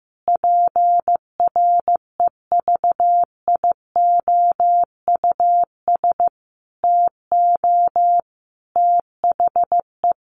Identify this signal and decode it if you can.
Morse code, heard as a keyed tone, about 15 wpm, 700 Hz. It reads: PREVIOUS TO THE
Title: Sentences from Top 1500 Words 15wpm